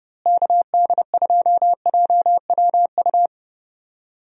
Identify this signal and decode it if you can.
Morse code, heard as a keyed tone, about 30 wpm, 700 Hz. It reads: KD2JWU
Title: USA Call Signs - 6 Characters - Repeated 3x - 30wpm